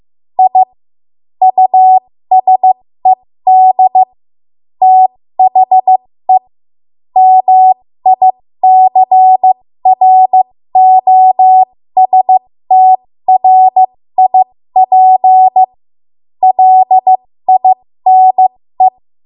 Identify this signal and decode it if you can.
Morse code, heard as a keyed tone, about 15 wpm, spaced as for 13 wpm, 750 Hz. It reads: I USED THE MICROSTRIP LINE
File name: Morse / 13WPM